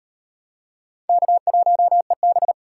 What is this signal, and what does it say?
Morse code, heard as a keyed tone, about 34 wpm, 700 Hz. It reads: K1EB